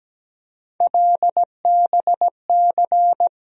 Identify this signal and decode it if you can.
Morse code, heard as a keyed tone, about 17 wpm, 700 Hz. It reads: LBC
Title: USA Call Sign Postfixes - 3 Letters 17wpm